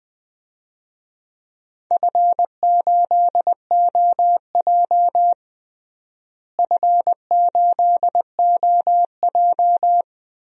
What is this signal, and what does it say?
Morse code, heard as a keyed tone, about 20 wpm, 700 Hz. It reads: F8OJ F8OJ
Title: DXCC-100 Call Signs - 4 Characters - Repeated 3x - 20wpm